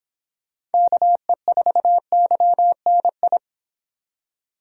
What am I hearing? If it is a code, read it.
Morse code, heard as a keyed tone, about 30 wpm, 700 Hz. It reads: KE4YNI